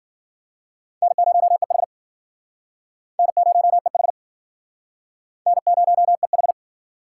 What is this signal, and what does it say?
Morse code, heard as a keyed tone, about 53 wpm, 700 Hz. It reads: N0EH N0EH N0EH